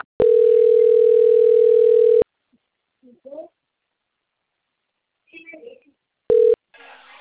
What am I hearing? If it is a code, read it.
Ringback tone (North America), 440 + 480 Hz, with speech.
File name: out- - - - - .65531